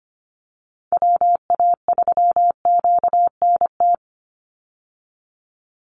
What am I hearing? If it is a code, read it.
Morse code, heard as a keyed tone, about 25 wpm, 700 Hz. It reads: WA3QNT
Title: Call Signs Repeated 3x - 25wpm